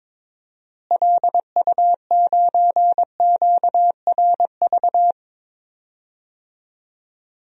Morse code, heard as a keyed tone, about 21 wpm, 700 Hz. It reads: LU9QRV
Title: DXCC-100 Call Signs - 6 Characters - Repeated 2x - 22wpm